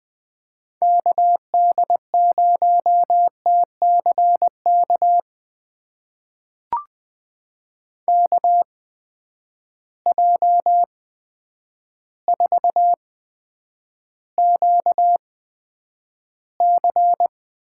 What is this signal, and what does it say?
Morse code, heard as a keyed tone, about 20 wpm, spaced as for 9 wpm, 700 Hz. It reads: KD0TCK KJ4QC
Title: USA Call Signs - 6 Characters - 8x Character Spacing - 20wpm